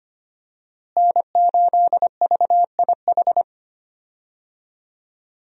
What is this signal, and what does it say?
Morse code, heard as a keyed tone, about 25 wpm, 700 Hz. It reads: N8VIH
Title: USA Call Signs - 5 Characters - Repeated 2x - 25wpm